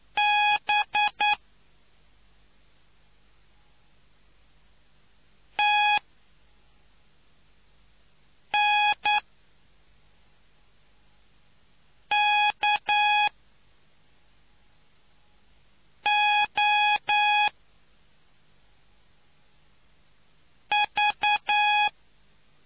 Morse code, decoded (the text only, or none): B T N K O V